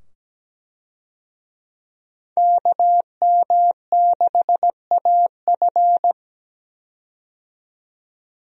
Morse code, heard as a keyed tone, about 17 wpm, 700 Hz. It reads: KM6AF
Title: USA Call Signs - 5 Characters - 17wpm